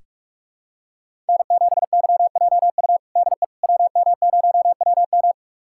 Morse code, heard as a keyed tone, about 46 wpm, 700 Hz. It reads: N7YJU DE WM0WM
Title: USA Callsign DE Callsign - 5 Characters - 45wpm